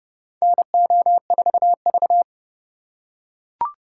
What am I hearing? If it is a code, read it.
Morse code, heard as a keyed tone, about 29 wpm, 700 Hz. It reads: NO4V
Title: USA Call Signs - 4 Characters - 3x Character Spacing - 30wpm